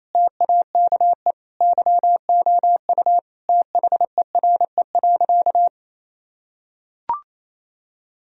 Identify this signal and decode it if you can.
Morse code, heard as a keyed tone, about 28 wpm, 700 Hz. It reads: TAKE YOU THERE.
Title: Sentences from Top 1000 Words - Review Part 3 - 28wpm